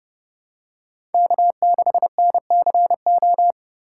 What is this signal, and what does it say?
Morse code, heard as a keyed tone, about 30 wpm, 700 Hz. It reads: K6NCO
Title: USA Call Signs - 5 Characters - 4x Character Spacing - 30wpm